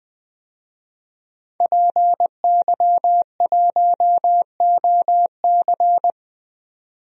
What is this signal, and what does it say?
Morse code, heard as a keyed tone, about 21 wpm, 700 Hz. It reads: PY1OC
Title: DXCC-100 Call Signs - 5 Characters - 20wpm